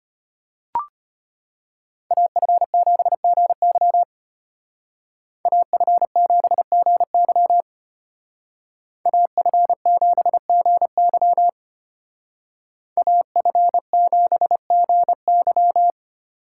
Morse code, heard as a keyed tone, about 33 wpm, 700 Hz. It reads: AF7GY AF7GY AF7GY AF7GY